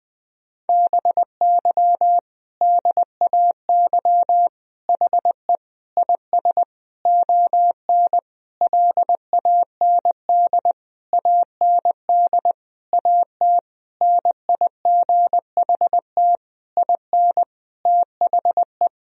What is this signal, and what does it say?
Morse code, heard as a keyed tone, about 20 wpm, 700 Hz. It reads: BY DAY HE IS ON LAND AND AT NIGHT IN THE
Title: Sentences from Top 1000 Words - Review Part 1 - 20wpm